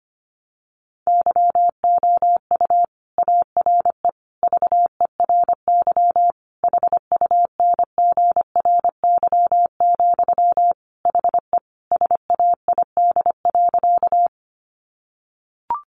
Morse code, heard as a keyed tone, about 25 wpm, 700 Hz. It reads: YOU ARE VERY HUNGRY, HE SAID.
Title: Sentences from Top 1400 Words 25wpm